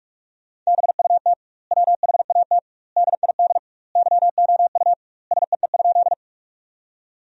Morse code, heard as a keyed tone, about 46 wpm, 700 Hz. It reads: BUT WHAT DID YOU SEE?
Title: 2x Word-Spacing - Sentences from Top 100 Words - 45wpm